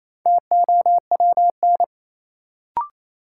Morse code, heard as a keyed tone, about 28 wpm, 700 Hz. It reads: TOWN
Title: Sentences from Top 500 Words - New Word Drill - 28wpm